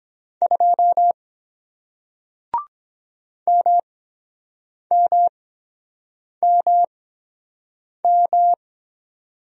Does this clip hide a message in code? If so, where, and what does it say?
Yes: Morse code from 0.4 to 8.5 s, heard as a keyed tone, about 23 wpm, spaced as for 7 wpm, 700 Hz. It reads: 2 MMMM